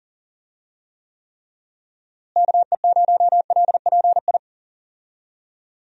Morse code, heard as a keyed tone, about 37 wpm, 700 Hz. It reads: KE0LPI
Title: USA Call Signs - 6 Characters - Repeated 3x - 40wpm